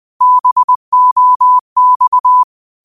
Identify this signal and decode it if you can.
Morse code, heard as a keyed tone, about 20 wpm, 1000 Hz. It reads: BOX
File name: qotdfiles / 20wpm